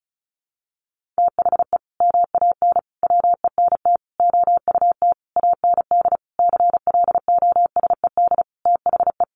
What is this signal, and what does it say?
Morse code, heard as a keyed tone, about 34 wpm, 700 Hz. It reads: THE MAN WENT OUT AND CLOSED THE
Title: Sentences from Top 1400 Words 35wpm